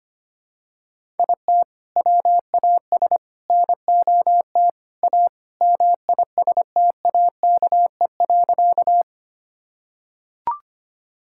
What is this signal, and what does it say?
Morse code, heard as a keyed tone, about 25 wpm, 700 Hz. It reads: IT WAS NOT A MISTAKE.
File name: practice / 25wpm / Sentences with 4-Word Phrases - 25wpm